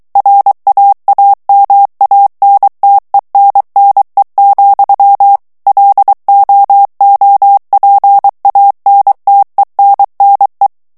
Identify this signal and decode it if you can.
Morse code, heard as a keyed tone, about 23 wpm, 800 Hz. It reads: RAAMANTENNE, LOOPANTENNE
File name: HOLLANDAIS texte clair 1 vitesse 22 mots minute